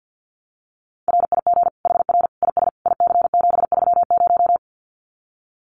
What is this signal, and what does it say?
Morse code, heard as a keyed tone, about 50 wpm, 700 Hz. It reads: RIG HR IS IC730
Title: Common QSO Phrases 50wpm